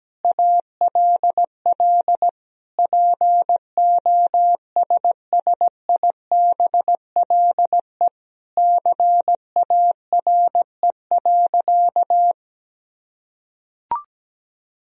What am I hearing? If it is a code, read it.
Morse code, heard as a keyed tone, about 17 wpm, 700 Hz. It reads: ALL POSSIBLE CARE.
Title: Sentences from Top 1000 Words - Review Part 2 - 17wpm